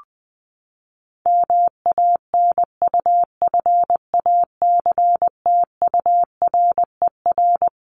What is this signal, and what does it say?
Morse code, heard as a keyed tone, about 20 wpm, 700 Hz. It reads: MANUFACTURER
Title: Top 1751-2000 Words 20wpm